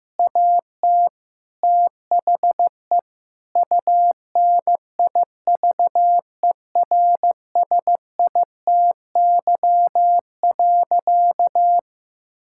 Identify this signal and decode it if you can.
Morse code, heard as a keyed tone, about 15 wpm, 700 Hz. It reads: AT THE UNIVERSITY.
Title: Sentences from Top 1000 Words - Review Part 5 - 15wpm